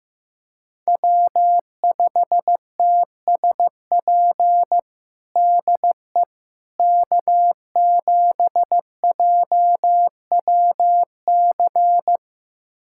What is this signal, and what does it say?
Morse code, heard as a keyed tone, about 15 wpm, 700 Hz. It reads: W5TSP DE K7JWC
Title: USA Callsign DE Callsign - 5 Characters - 15wpm